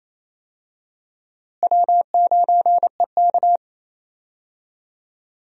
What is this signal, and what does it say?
Morse code, heard as a keyed tone, about 28 wpm, 700 Hz. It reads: W9EK
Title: USA Call Signs - 4 Characters - Repeated 2x - 28wpm